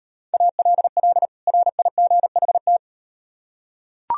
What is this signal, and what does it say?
Morse code, heard as a keyed tone, about 38 wpm, 700 Hz. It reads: ALL RIGHT